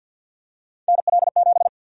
Morse code, heard as a keyed tone, about 50 wpm, 700 Hz. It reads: NL7
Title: USA Call Sign Prefixes - Repeated 2x - 50wpm